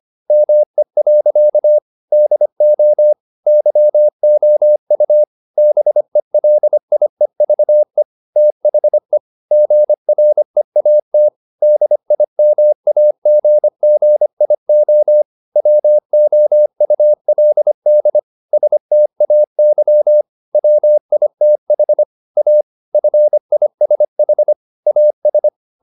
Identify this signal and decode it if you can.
Morse code, heard as a keyed tone, about 25 wpm, 600 Hz. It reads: ME. DO YOU BELIEVE THE GREAT DIMAGGIO WOULD STAY WITH A FISH AS
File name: the old man and the sea part 1 25wpm